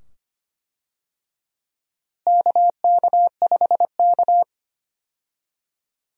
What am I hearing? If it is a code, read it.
Morse code, heard as a keyed tone, about 25 wpm, 700 Hz. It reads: KK5K
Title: USA Call Signs - 4 Characters - Repeated 2x - 25wpm